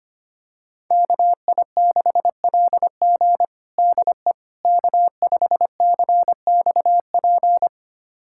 Morse code, heard as a keyed tone, about 25 wpm, 700 Hz. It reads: KI6LG DE K5CXP
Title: USA Callsign DE Callsign - 5 Characters - 25wpm